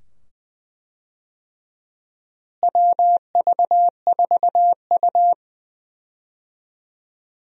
Morse code, heard as a keyed tone, about 20 wpm, 700 Hz. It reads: WV4U
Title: USA Call Signs - 4 Characters - Repeated 2x - 20wpm